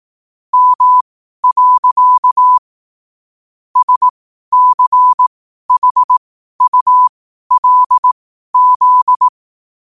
Morse code, heard as a keyed tone, about 18 wpm, spaced as for 12 wpm, 1000 Hz. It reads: M. SCHULZ
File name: qotdfiles / 12wpm